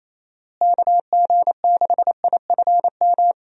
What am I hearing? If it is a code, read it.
Morse code, heard as a keyed tone, about 28 wpm, 700 Hz. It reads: KG6IFM
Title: USA Call Signs - 6 Characters - Repeated 2x - 28wpm